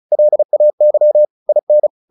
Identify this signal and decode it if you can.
Morse code, heard as a keyed tone, about 34 wpm, 600 Hz. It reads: LAY IN
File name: the old man and the sea / the old man and the sea 1 35wpm